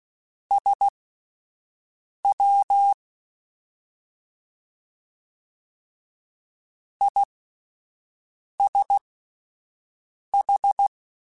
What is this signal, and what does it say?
Morse code, heard as a keyed tone, about 16 wpm, spaced as for 5 wpm, 800 Hz. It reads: SW ISH
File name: Misc / Misc-mp3s / Jumps @5